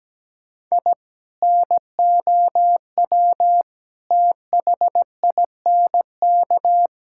Morse code, heard as a keyed tone, about 17 wpm, 700 Hz. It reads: I NOW THINK
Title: Sentences from Top 1000 Words - Review Part 5 - 17wpm